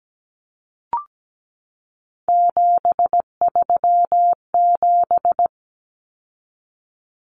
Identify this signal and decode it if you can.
Morse code, heard as a keyed tone, about 17 wpm, 700 Hz. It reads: 737
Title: Common Cut Numbers - 3-Digit 17wpm